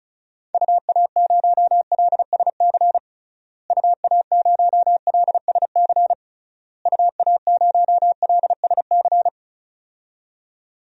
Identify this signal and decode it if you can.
Morse code, heard as a keyed tone, about 34 wpm, 700 Hz. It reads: UA0LSC UA0LSC UA0LSC
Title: DXCC-100 Call Signs - 6 Characters - Repeated 3x - 35wpm